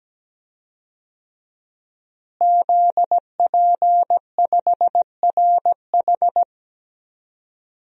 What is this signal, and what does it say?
Morse code, heard as a keyed tone, about 17 wpm, 700 Hz. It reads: ZP5RH
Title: DXCC-100 Call Signs - 5 Characters - Repeated 3x - 17wpm